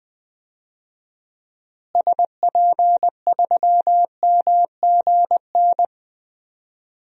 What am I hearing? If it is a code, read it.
Morse code, heard as a keyed tone, about 21 wpm, 700 Hz. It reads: SP3MGN
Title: DXCC-100 Call Signs - 6 Characters - Repeated 3x - 20wpm